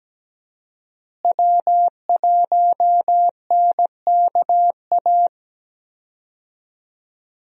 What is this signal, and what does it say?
Morse code, heard as a keyed tone, about 17 wpm, 700 Hz. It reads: W1NKA
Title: USA Call Signs - 5 Characters - Repeated 3x - 17wpm